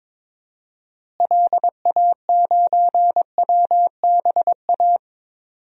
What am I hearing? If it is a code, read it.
Morse code, heard as a keyed tone, about 22 wpm, 700 Hz. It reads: LA9WBA
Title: DXCC-100 Call Signs - 6 Characters - Repeated 3x - 22wpm